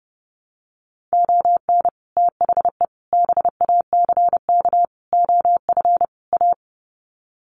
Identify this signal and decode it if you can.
Morse code, heard as a keyed tone, about 30 wpm, 700 Hz. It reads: ON THE BACK OF A
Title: Sets of 5 Words Encore2 30wpm